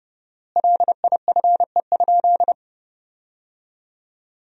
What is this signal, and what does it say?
Morse code, heard as a keyed tone, about 30 wpm, 700 Hz. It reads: LIFE?
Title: 4x Word-Spacing - Sentences from Top 300 Words - 30wpm